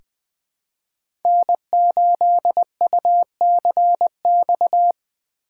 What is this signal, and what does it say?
Morse code, heard as a keyed tone, about 20 wpm, 700 Hz. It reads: N8UCX
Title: USA Call Signs - 5 Characters - 4x Character Spacing - 20wpm